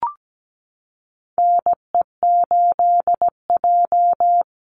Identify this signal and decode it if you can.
Morse code, heard as a keyed tone, about 17 wpm, 700 Hz. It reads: NE8J
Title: Call Signs - 17wpm